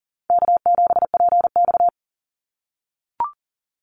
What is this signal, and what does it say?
Morse code, heard as a keyed tone, about 40 wpm, 700 Hz. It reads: K7PX
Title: NA SOTA Activator Callsigns 40wpm